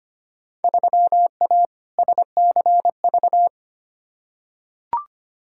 Morse code, heard as a keyed tone, about 25 wpm, 700 Hz. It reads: 3A SCV
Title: Field Day Exchanges - 25wpm